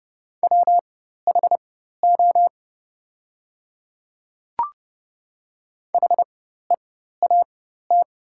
Morse code, heard as a keyed tone, about 30 wpm, spaced as for 14 wpm, 700 Hz. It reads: WHO HEAT
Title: 4x Character Spacing - Top 500 Words - 30wpm